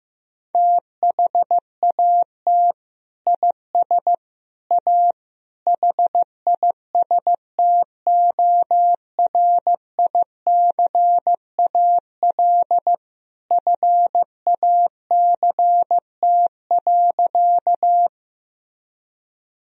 Morse code, heard as a keyed tone, about 15 wpm, 700 Hz. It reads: THAT IS A HISTORICAL FACT.
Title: Sentences from Top 1000 Words - Review Part 4 - 15wpm